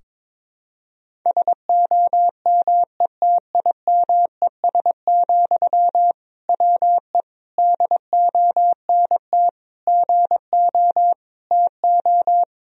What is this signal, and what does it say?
Morse code, heard as a keyed tone, about 22 wpm, 700 Hz. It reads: SOMETIMES, WE DONT GO TO
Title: Sentences from Top 1000 Words - Review Part 3 - 22wpm